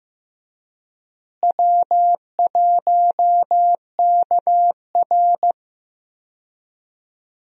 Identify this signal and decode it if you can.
Morse code, heard as a keyed tone, about 15 wpm, 700 Hz. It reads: W1KR
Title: USA Call Signs - 4 Characters - Repeated 3x - 15wpm